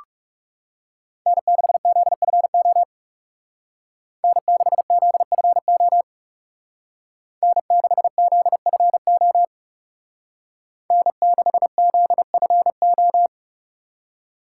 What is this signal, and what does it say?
Morse code, heard as a keyed tone, about 36 wpm, 700 Hz. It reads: N6ZFO N6ZFO N6ZFO N6ZFO
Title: Speed Racing - USA 1x3 Call Signs 30wpm